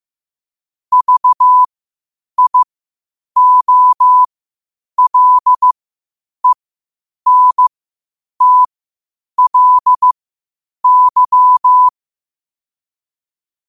Morse code, heard as a keyed tone, about 15 wpm, spaced as for 8 wpm, 1000 Hz. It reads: VIOLENTLY
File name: qotdfiles / 8wpm